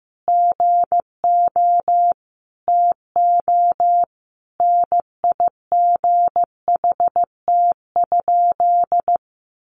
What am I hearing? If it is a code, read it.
Morse code, heard as a keyed tone, about 15 wpm, 700 Hz. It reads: GO TO NIGHT?
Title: Sentences from Top 600 Words 15wpm